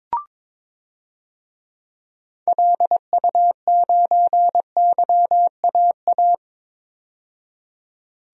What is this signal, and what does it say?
Morse code, heard as a keyed tone, about 22 wpm, 700 Hz. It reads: LU9YAA
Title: DXCC-100 Call Signs - 6 Characters - 22wpm